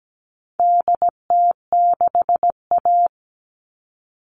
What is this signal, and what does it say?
Morse code, heard as a keyed tone, about 17 wpm, 700 Hz. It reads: DT6A
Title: All Cut Numbers - 4-Digit 17wpm